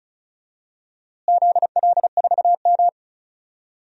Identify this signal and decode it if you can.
Morse code, heard as a keyed tone, about 36 wpm, 700 Hz. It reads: ZL4M
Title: DXCC-100 Call Signs - 4 Characters - Repeated 2x - 35wpm